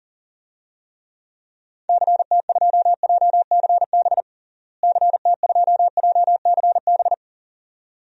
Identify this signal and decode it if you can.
Morse code, heard as a keyed tone, about 40 wpm, 700 Hz. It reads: CT2JCB CT2JCB
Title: DXCC-100 Call Signs - 6 Characters - Repeated 2x - 40wpm